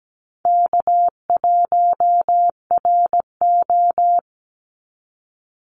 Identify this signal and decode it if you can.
Morse code, heard as a keyed tone, about 17 wpm, 700 Hz. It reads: K1RO
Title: Call Signs Repeated 3x - 17wpm